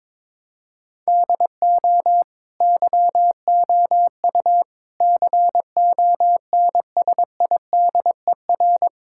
Morse code, heard as a keyed tone, about 22 wpm, 700 Hz. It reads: DO YOU CONSIDER
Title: Sentences from Top 1000 Words - Review Part 6 - 22wpm